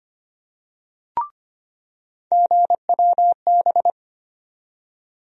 Morse code, heard as a keyed tone, about 25 wpm, 700 Hz. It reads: GWB
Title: USA Call Sign Postfixes - 3 Letters 25wpm